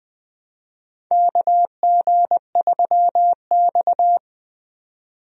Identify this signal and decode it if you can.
Morse code, heard as a keyed tone, about 20 wpm, 700 Hz. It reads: KG3X
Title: USA Call Signs - 4 Characters - 20wpm